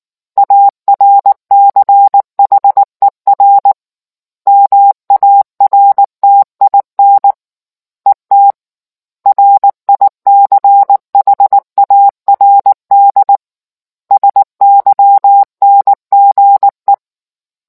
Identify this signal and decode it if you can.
Morse code, heard as a keyed tone, about 19 wpm, 800 Hz. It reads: ARCHER MARTIN ET RICHARD SYNGE